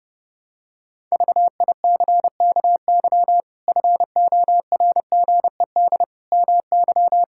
Morse code, heard as a keyed tone, about 30 wpm, 700 Hz. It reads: VICKY FORGED MY
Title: Pangram Challenge v2 - 30wpm